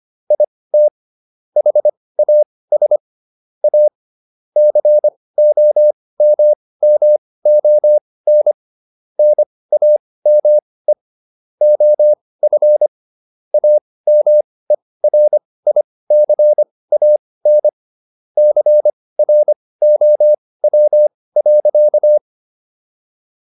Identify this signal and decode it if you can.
Morse code, heard as a keyed tone, about 25 wpm, spaced as for 18 wpm, 600 Hz. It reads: IT HAS A COMMON NAME OF AMERICAN CROW.